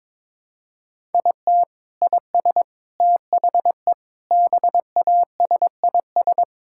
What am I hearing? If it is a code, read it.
Morse code, heard as a keyed tone, about 22 wpm, 700 Hz. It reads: IT IS THE BASIS
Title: Sentences from Top 1000 Words - Review Part 5 - 22wpm